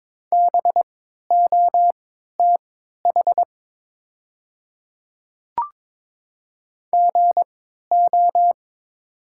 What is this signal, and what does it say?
Morse code, heard as a keyed tone, about 22 wpm, spaced as for 12 wpm, 690 Hz. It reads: BOTH GO
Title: 3x Character Spacing - Top 300 Words - 22wpm